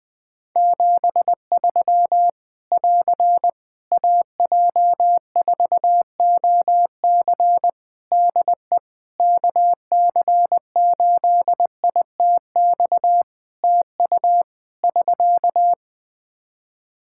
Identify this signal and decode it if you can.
Morse code, heard as a keyed tone, about 20 wpm, 700 Hz. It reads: 73 <AR> AJ4OC DE KC8ITX TU <SK>
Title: Practice QSO - Exchange 4 - Ending - 20wpm